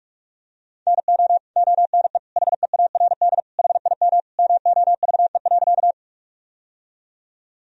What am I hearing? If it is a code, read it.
Morse code, heard as a keyed tone, about 46 wpm, 700 Hz. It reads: NO ONE HEARD HIM MOVE.
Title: Sentences from Top 1000 Words - Review Part 1 - 45wpm